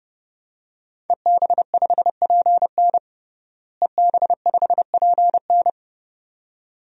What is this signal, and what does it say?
Morse code, heard as a keyed tone, about 30 wpm, 700 Hz. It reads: EB5PN EB5PN
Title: DXCC-100 Call Signs - 5 Characters - Repeated 2x - 30wpm